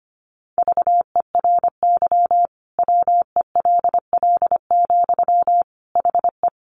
Morse code, heard as a keyed tone, about 25 wpm, 700 Hz. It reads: VERY WELL, HE
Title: Sentences from Top 200 Words 25wpm